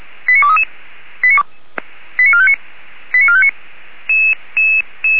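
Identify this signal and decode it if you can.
Five-tone selective call (ZVEI), heard as quick runs of five tones, tones of about 70 ms. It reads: 89233 89459 89459
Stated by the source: ZVEI five-tone selective call received on an SDR.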